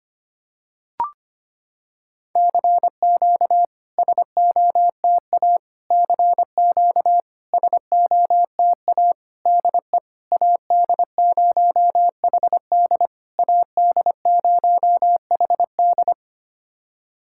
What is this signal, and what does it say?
Morse code, heard as a keyed tone, about 25 wpm, 700 Hz. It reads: CQ SOTA CQ SOTA DE AD0HD AD0HD
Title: SOTA QSO Practice 25wpm